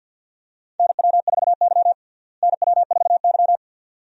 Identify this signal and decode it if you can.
Morse code, heard as a keyed tone, about 47 wpm, 690 Hz. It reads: NW4Y NW4Y
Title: USA Call Signs - 4 Characters - Repeated 2x - 50wpm